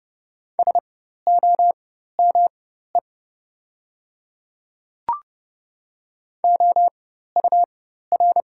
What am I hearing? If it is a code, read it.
Morse code, heard as a keyed tone, about 30 wpm, spaced as for 14 wpm, 700 Hz. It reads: SOME OUR
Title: 4x Character Spacing - Top 200 Words - 30wpm